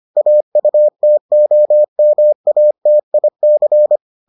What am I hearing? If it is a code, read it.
Morse code, heard as a keyed tone, about 25 wpm, 600 Hz. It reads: AUTOMATIC